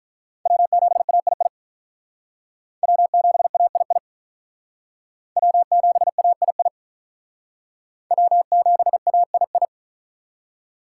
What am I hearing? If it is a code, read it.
Morse code, heard as a keyed tone, about 45 wpm, 700 Hz. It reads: W7AII W7AII W7AII W7AII